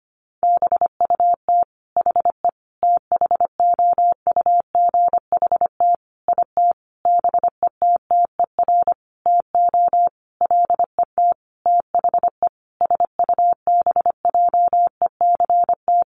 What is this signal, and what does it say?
Morse code, heard as a keyed tone, about 25 wpm, 700 Hz. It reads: BUT HE THOUGHT IT BETTER TO LET THE SUBJECT
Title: Sentences from Top 1400 Words 25wpm